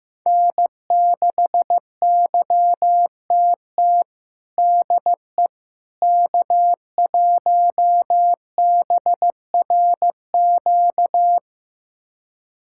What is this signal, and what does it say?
Morse code, heard as a keyed tone, about 15 wpm, 700 Hz. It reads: N6YTT DE K1BRQ
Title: USA Callsign DE Callsign 15wpm